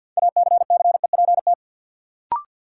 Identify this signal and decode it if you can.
Morse code, heard as a keyed tone, about 48 wpm, 700 Hz. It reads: ACCEPT
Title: Sentences from Top 1300 Words - New Word Drill - 50wpm